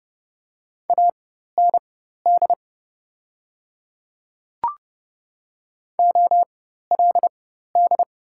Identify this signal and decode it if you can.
Morse code, heard as a keyed tone, about 30 wpm, spaced as for 14 wpm, 700 Hz. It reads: AND OLD
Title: 4x Character Spacing - Top 200 Words - 30wpm